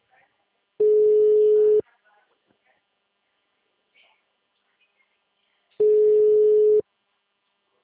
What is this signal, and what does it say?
Ringback tone (Europe, 425 Hz), with speech.
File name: out- -2003- - - .92483